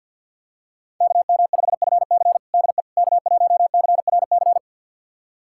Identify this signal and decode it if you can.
Morse code, heard as a keyed tone, about 50 wpm, 700 Hz. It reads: KM5FC DE K1CRC
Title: USA Callsign DE Callsign - 5 Characters - 50wpm